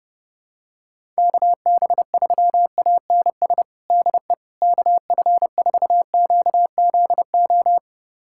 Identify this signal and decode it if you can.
Morse code, heard as a keyed tone, about 30 wpm, 700 Hz. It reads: KB3ANS DE KF4QZO
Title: USA Callsign DE Callsign - 6 Characters - 30wpm